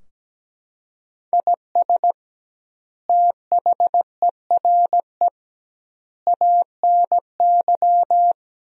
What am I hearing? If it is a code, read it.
Morse code, heard as a keyed tone, about 17 wpm, 700 Hz. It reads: IS THERE ANY
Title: 2x Word-Spacing - Sentences from Top 300 Words - 17wpm